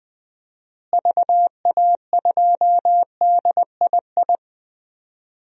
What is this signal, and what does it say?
Morse code, heard as a keyed tone, about 20 wpm, 700 Hz. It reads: VA2DII
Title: DXCC-100 Call Signs - 6 Characters - Repeated 2x - 20wpm